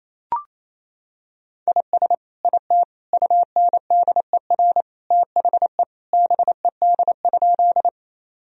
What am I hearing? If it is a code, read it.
Morse code, heard as a keyed tone, about 28 wpm, 700 Hz. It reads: IS IT UNDER THE BED?
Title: Sentences from Top 1000 Words - Review Part 1 - 28wpm